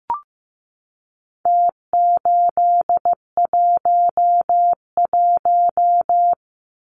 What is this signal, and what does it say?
Morse code, heard as a keyed tone, about 15 wpm, 700 Hz. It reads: T811
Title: Numbers - Contest Style - 4-Digit 15wpm